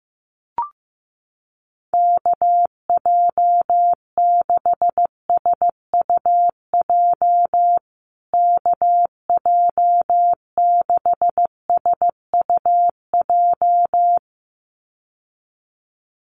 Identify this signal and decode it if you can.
Morse code, heard as a keyed tone, about 15 wpm, 700 Hz. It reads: KJ6SUJ KJ6SUJ
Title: USA Call Signs Repeated 2x 15wpm